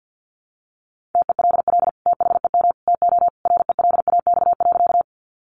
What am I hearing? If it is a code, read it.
Morse code, heard as a keyed tone, about 50 wpm, 700 Hz. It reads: TELL THEM TO RELAX.
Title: Sentences from Top 1400 Words 50wpm